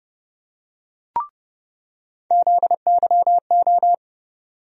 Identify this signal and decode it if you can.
Morse code, heard as a keyed tone, about 30 wpm, 700 Hz. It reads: ZYO
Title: USA Call Sign Postfixes - 3 Letters 30wpm